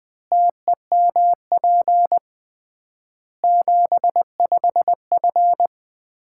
Morse code, heard as a keyed tone, about 20 wpm, 700 Hz. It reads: TEMP 75F
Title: Common QSO Phrases - 3x Word-Spacing - 20wpm